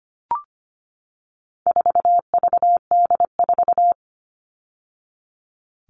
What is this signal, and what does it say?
Morse code, heard as a keyed tone, about 25 wpm, 700 Hz. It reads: 4VD4
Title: All Cut Numbers - 4-Digit 25wpm